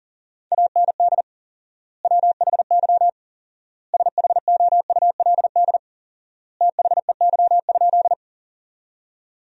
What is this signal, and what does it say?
Morse code, heard as a keyed tone, about 40 wpm, 700 Hz. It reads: AND WHY SHOULD THEY?
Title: 4x Word-Spacing - Sentences from Top 300 Words - 40wpm